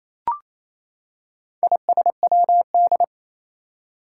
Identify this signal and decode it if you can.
Morse code, heard as a keyed tone, about 28 wpm, 700 Hz. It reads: ISWD
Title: Four Challenging Letter-Number Combinations 28wpm